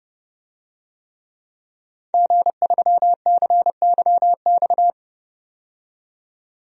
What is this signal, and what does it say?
Morse code, heard as a keyed tone, about 30 wpm, 700 Hz. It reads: G3CYX
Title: DXCC-100 Call Signs - 5 Characters - 30wpm